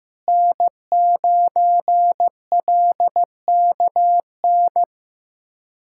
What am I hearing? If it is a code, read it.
Morse code, heard as a keyed tone, about 15 wpm, 700 Hz. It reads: N9LKN
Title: USA Callsign DE Callsign - 5 Characters - 15wpm